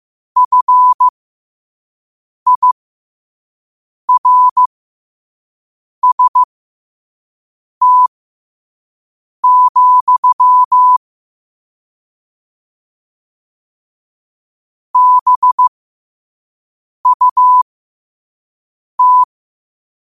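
Morse code, heard as a keyed tone, about 15 wpm, spaced as for 5 wpm, 1000 Hz. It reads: FIRST, BUT
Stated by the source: Morse quote-of-the-day practice at 5 wpm.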